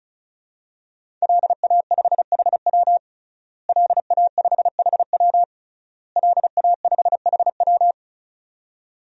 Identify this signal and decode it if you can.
Morse code, heard as a keyed tone, about 36 wpm, 700 Hz. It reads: LA5HW LA5HW LA5HW
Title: DXCC-100 Call Signs - 5 Characters - Repeated 3x - 35wpm